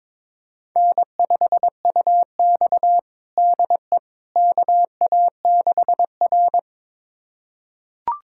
Morse code, heard as a keyed tone, about 21 wpm, 700 Hz. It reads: N5UX DE KA6R
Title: USA Callsign DE Callsign - 4 Characters - 22wpm